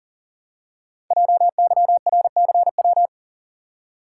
Morse code, heard as a keyed tone, about 41 wpm, 700 Hz. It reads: JYRCW
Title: Code Groups - 5 Letter:Numbers - 40wpm